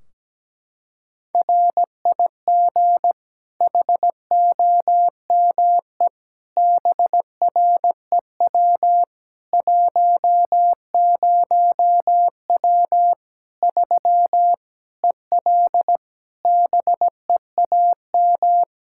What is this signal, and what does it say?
Morse code, heard as a keyed tone, about 17 wpm, 700 Hz. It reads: RIG HOME BREW 10W 3 EL BEAM
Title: Practice QSO - Exchange 2 - Info - 17wpm